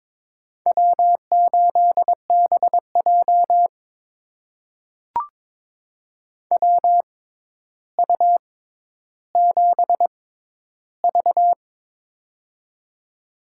Morse code, heard as a keyed tone, about 22 wpm, spaced as for 11 wpm, 700 Hz. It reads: W8BJ WU7V
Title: USA Call Signs - 6x Character Spacing - 22wpm